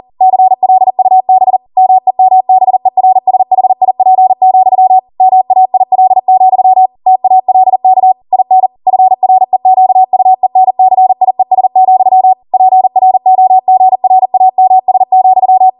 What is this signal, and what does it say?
Morse code, heard as a keyed tone, about 40 wpm, 750 Hz. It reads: CLUB MEMBERSHIP, MAIL, TALK IN FREQUENCIES, PROGRAMS,